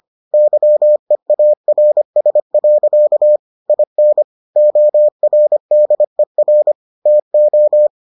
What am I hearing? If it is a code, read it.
Morse code, heard as a keyed tone, about 25 wpm, 600 Hz. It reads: YEARS. IN ORDER TO